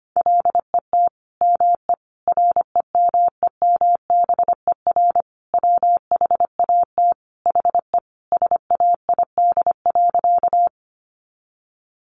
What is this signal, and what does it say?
Morse code, heard as a keyed tone, about 25 wpm, 700 Hz. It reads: LET ME REMEMBER WHAT HE SAID.
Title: Sentences from Top 400 Words 25wpm